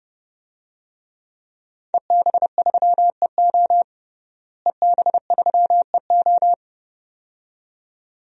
Morse code, heard as a keyed tone, about 32 wpm, 700 Hz. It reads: EB3EO EB3EO
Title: DXCC-100 Call Signs - 5 Characters - Repeated 2x - 30wpm